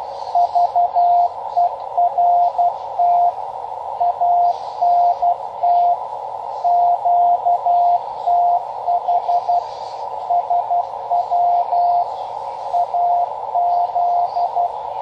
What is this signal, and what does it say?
Morse code, heard as a keyed tone, about 12 wpm, 700 Hz, with radio noise. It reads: VERT ANT QTH SW AZ
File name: Half Watt 10 Meter Beacon